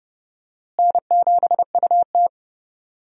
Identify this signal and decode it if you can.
Morse code, heard as a keyed tone, about 30 wpm, 700 Hz. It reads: N7UT
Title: USA Call Signs - 4 Characters - Repeated 2x - 30wpm